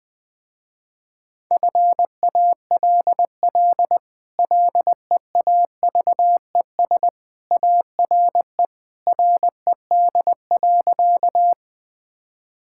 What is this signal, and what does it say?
Morse code, heard as a keyed tone, about 20 wpm, 700 Hz. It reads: FALL LEAVES ARE RED.
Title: Easy Sentences with Hints - 20wpm